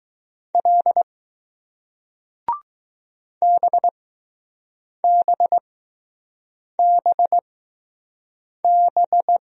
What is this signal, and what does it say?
Morse code, heard as a keyed tone, about 20 wpm, spaced as for 6 wpm, 700 Hz. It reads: L BBBB